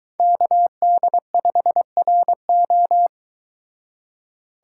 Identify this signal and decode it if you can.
Morse code, heard as a keyed tone, about 22 wpm, 700 Hz. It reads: KD5RO